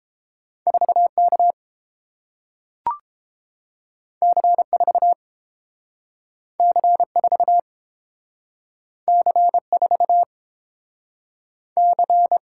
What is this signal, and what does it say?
Morse code, heard as a keyed tone, about 30 wpm, 700 Hz. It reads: 4 K C 4 C 4 C 4 C